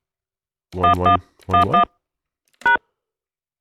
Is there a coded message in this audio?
Yes: DTMF from 0.8 to 2.8 s, heard as telephone touch tones, tones of about 120 ms. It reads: 1111 #